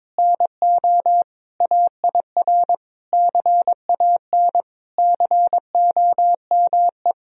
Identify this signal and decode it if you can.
Morse code, heard as a keyed tone, about 22 wpm, 700 Hz. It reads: NO AIR CAN COME
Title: Sentences from Top 1000 Words - Review Part 1 - 22wpm